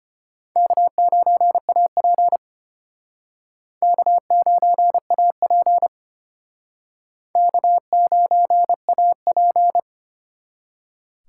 Morse code, heard as a keyed tone, about 30 wpm, 700 Hz. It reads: K9AP K9AP K9AP
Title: Speed Racing - USA Call Signs 25wpm